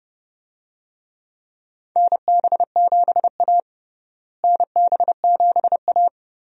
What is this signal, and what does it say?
Morse code, heard as a keyed tone, about 31 wpm, 700 Hz. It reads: NB7A NB7A
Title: USA Call Signs - 4 Characters - Repeated 2x - 30wpm